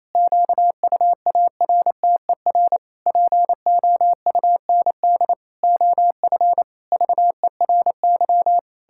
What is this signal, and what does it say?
Morse code, heard as a keyed tone, about 27 wpm, 700 Hz. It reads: QUARTER POUND OF VERY
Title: Pangram Challenge v2 - 28wpm